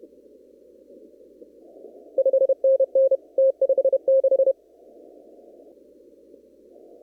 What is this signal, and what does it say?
Morse code, heard as a keyed tone, about 30 wpm, 530 Hz. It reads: 5NN T56
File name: CW / kt5j wpx cw-op-k5pi